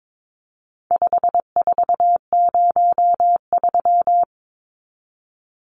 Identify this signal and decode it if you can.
Morse code, heard as a keyed tone, about 22 wpm, 700 Hz. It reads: 5403
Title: Numbers - 4-Digit 22wpm